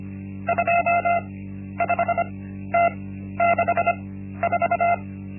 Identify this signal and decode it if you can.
Morse code, heard as a keyed tone, about 25 wpm, spaced as for 12 wpm, 690 Hz. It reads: 25T64
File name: Linksmakalnis-1